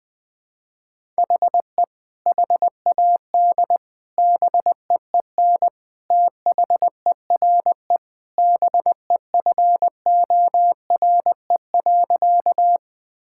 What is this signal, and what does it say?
Morse code, heard as a keyed tone, about 20 wpm, 700 Hz. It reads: HE HAD BEEN THERE BEFORE.
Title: Sentences with 3-Word Phrases - Spoken Beginning - 20wpm